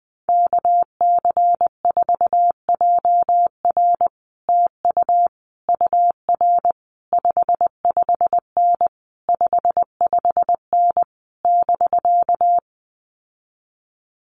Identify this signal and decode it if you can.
Morse code, heard as a keyed tone, about 20 wpm, 700 Hz. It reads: KC4JR TU UR 55N 55N <BK>
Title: Common QSO Phrases Encore 20wpm